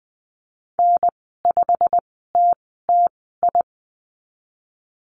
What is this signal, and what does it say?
Morse code, heard as a keyed tone, about 20 wpm, spaced as for 15 wpm, 700 Hz. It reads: N5TTI
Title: 088 Lesson 15g - Number 5 - Call Signs 20wpm